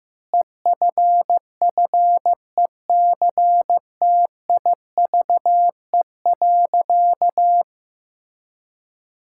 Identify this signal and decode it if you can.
Morse code, heard as a keyed tone, about 15 wpm, 700 Hz. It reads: EFFECTIVE.
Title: Sentences from Top 1000 Words - Review Part 6 - 15wpm